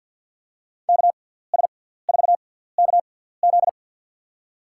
Morse code, heard as a keyed tone, about 47 wpm, spaced as for 17 wpm, 700 Hz. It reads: KS4KZ
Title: USA Call Signs - 6x Character Spacing - 50wpm